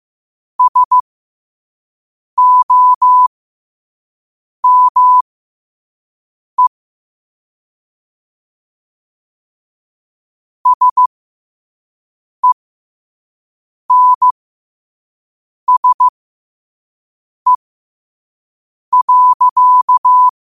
Morse code, heard as a keyed tone, about 15 wpm, spaced as for 5 wpm, 1000 Hz. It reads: SOME SENSE.
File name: qotdfiles / 5wpm